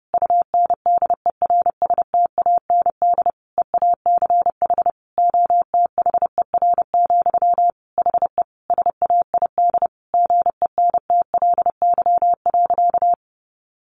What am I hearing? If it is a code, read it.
Morse code, heard as a keyed tone, about 30 wpm, 700 Hz. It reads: UNDERSTAND EACH OTHER, HE SAID GENTLY.
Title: Sentences from Top 1500 Words 30wpm